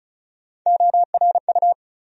Morse code, heard as a keyed tone, about 34 wpm, 700 Hz. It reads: ORU
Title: USA Call Sign Postfixes - 3 Letters 35wpm